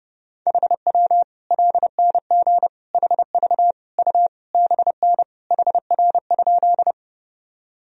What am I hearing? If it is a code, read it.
Morse code, heard as a keyed tone, about 30 wpm, 700 Hz. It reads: HW LNG HV U BN HR?
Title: Sentences from Top 200 Words and Abbreviations 30wpm